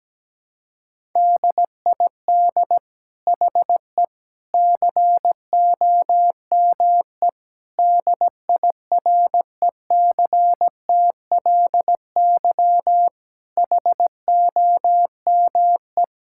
Morse code, heard as a keyed tone, about 17 wpm, 700 Hz. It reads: DID HE COME DIRECTLY HOME
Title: Sentences from Top 1000 Words - Review Part 5 - 17wpm